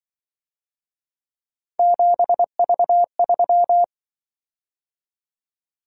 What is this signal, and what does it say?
Morse code, heard as a keyed tone, about 24 wpm, 700 Hz. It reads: 7V3
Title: Speed-Racing - Three Letter-Number Combinations - 20wpm